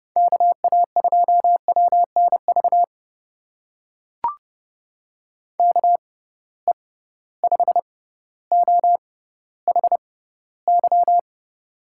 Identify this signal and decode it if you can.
Morse code, heard as a keyed tone, about 30 wpm, spaced as for 15 wpm, 700 Hz. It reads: KA2WNV KE5OHY
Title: USA Call Signs - 6 Characters - 6x Character Spacing - 30wpm